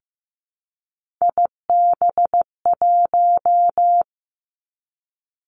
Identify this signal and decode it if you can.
Morse code, heard as a keyed tone, about 15 wpm, 700 Hz. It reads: IB1
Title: Three Challenging Letter-Number Combinations 15wpm